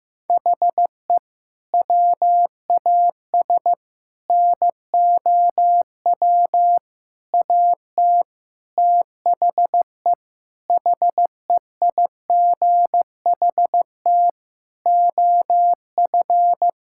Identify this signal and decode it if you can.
Morse code, heard as a keyed tone, about 15 wpm, 700 Hz. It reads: HE WAS NOW AT THE HEIGHT OF
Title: Sentences from Top 1600 Words 15wpm